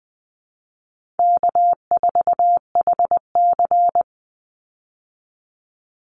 Morse code, heard as a keyed tone, about 20 wpm, 700 Hz. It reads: K4HC
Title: Easy Call Signs - 20wpm